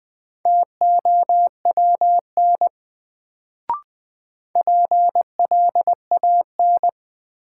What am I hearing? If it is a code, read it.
Morse code, heard as a keyed tone, about 20 wpm, 700 Hz. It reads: TOWN PLAN
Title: Sentences from Top 500 Words - New Word Drill - 20wpm